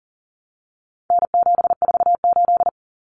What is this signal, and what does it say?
Morse code, heard as a keyed tone, about 40 wpm, 700 Hz. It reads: N748
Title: Common Cut Numbers - 4-Digit 40wpm